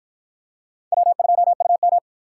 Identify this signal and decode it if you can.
Morse code, heard as a keyed tone, about 53 wpm, 700 Hz. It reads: W2UM